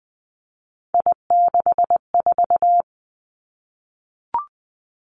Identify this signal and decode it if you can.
Morse code, heard as a keyed tone, about 20 wpm, 700 Hz. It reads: I64
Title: Three Challenging Letter-Number Combinations 20wpm